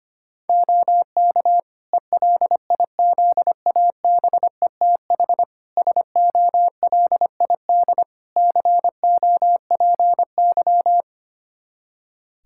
Morse code, heard as a keyed tone, about 25 wpm, 700 Hz. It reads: OK ELIZABETH SOLID COPY
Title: Practice QSO - Exchange 3 - More Info - 25wpm